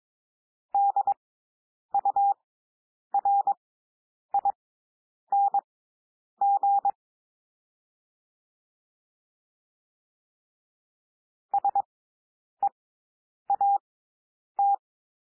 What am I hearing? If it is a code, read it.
Morse code, heard as a keyed tone, about 22 wpm, spaced as for 9 wpm, 800 Hz. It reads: DURING SEAT
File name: SOUND WORDS 22 wpm